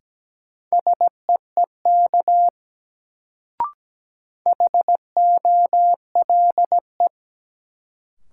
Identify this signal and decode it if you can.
Morse code, heard as a keyed tone, about 17 wpm, 700 Hz. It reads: SEEK HOLE
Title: Sentences from Top 1600 Words - New Word Drill - 17wpm